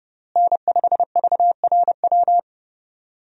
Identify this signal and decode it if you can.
Morse code, heard as a keyed tone, about 30 wpm, 700 Hz. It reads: N5VRW
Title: USA Call Signs - 5 Characters - Repeated 3x - 30wpm